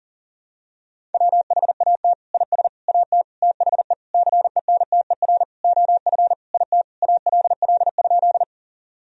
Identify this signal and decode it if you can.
Morse code, heard as a keyed tone, about 40 wpm, 700 Hz. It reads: WHAT IS AT THE CENTER OF IT ALL?
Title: Sentences from Top 1000 Words - Review Part 2 - 40wpm